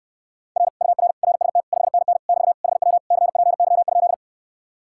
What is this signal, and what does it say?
Morse code, heard as a keyed tone, about 69 wpm, 700 Hz. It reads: R UR RST 5NN <BT> HW CPY?